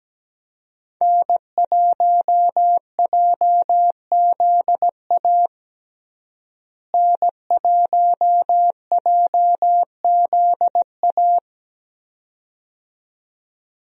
Morse code, heard as a keyed tone, about 17 wpm, 700 Hz. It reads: N1JZA N1JZA
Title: USA Call Signs - 5 Characters - Repeated 2x - 17wpm